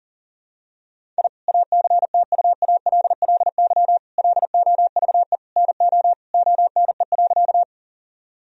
Morse code, heard as a keyed tone, about 40 wpm, 700 Hz. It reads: I ACTUALLY LOVE NO ONE.
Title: Sentences from Top 1000 Words - Review Part 4 - 40wpm